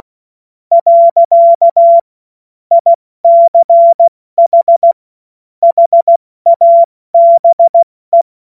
Morse code, heard as a keyed tone, about 16 wpm, 680 Hz. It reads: . ICH HABE
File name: Morsekurs / Witze / Witz 06. September 2022 70 F80